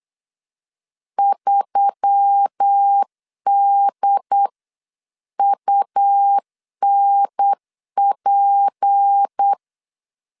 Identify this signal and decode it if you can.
Morse code, heard as a keyed tone, about 8 wpm, 800 Hz. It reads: 3DUNP
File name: 8wpm